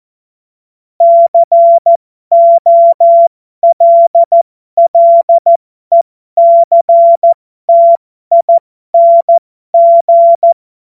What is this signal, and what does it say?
Morse code, heard as a keyed tone, about 14 wpm, spaced as for 12 wpm, 680 Hz. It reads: COLLECTING
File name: Morsekurs / Witze